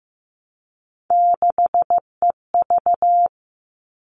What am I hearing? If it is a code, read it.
Morse code, heard as a keyed tone, about 15 wpm, 700 Hz. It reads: 6EV
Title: All Cut Numbers - 3-Digit 15wpm